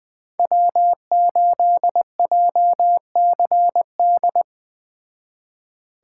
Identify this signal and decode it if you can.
Morse code, heard as a keyed tone, about 20 wpm, 700 Hz. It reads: W8JCD
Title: Speed Racing - USA 1x3 Call Signs 17wpm